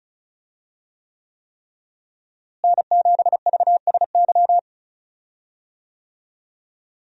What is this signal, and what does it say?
Morse code, heard as a keyed tone, about 36 wpm, 700 Hz. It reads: N7VSY
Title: USA Call Signs - 5 Characters - Repeated 3x - 35wpm